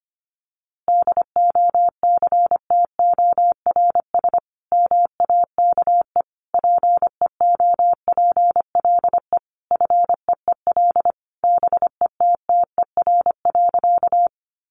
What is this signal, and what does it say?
Morse code, heard as a keyed tone, about 25 wpm, 700 Hz. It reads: DOCTORS MAKE PEOPLE FEEL BETTER.
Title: Easy Phrases 25wpm